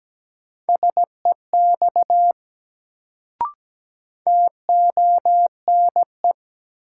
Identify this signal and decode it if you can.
Morse code, heard as a keyed tone, about 17 wpm, 700 Hz. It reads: SEX TONE
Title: Sentences from Top 1600 Words - New Word Drill - 17wpm